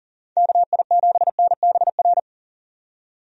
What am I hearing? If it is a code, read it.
Morse code, heard as a keyed tone, about 40 wpm, 700 Hz. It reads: KI7NBR
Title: USA Call Signs - 6 Characters - 40wpm